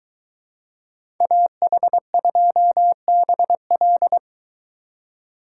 Morse code, heard as a keyed tone, about 23 wpm, 700 Hz. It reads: AH2BL